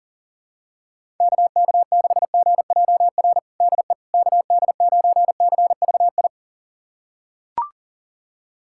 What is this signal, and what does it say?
Morse code, heard as a keyed tone, about 40 wpm, 700 Hz. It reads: KK6GJR DE KD9CVI
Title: USA Callsign DE Callsign - 6 Characters - 40wpm